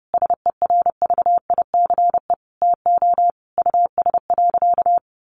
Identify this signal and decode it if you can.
Morse code, heard as a keyed tone, about 30 wpm, 700 Hz. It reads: SERVICE TO US.
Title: Sentences from Top 1200 Words 30wpm